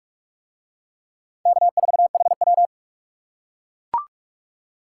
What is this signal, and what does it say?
Morse code, heard as a keyed tone, about 44 wpm, 700 Hz. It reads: K4HW